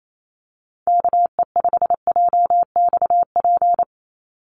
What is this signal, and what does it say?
Morse code, heard as a keyed tone, about 28 wpm, 700 Hz. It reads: KE5JXP
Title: USA Call Signs 28wpm